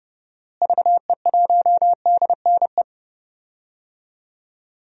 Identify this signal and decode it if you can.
Morse code, heard as a keyed tone, about 30 wpm, 700 Hz. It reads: VE1DNE
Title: DXCC-100 Call Signs - 6 Characters - Repeated 2x - 30wpm